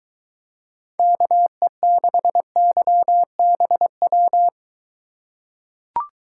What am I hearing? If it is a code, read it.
Morse code, heard as a keyed tone, about 22 wpm, 700 Hz. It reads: KE6YBW